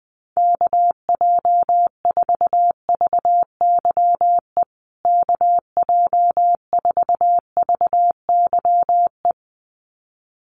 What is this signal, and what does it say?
Morse code, heard as a keyed tone, about 20 wpm, 700 Hz. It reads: KJ4VYE KJ4VYE
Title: USA Call Signs Repeated 2x 20wpm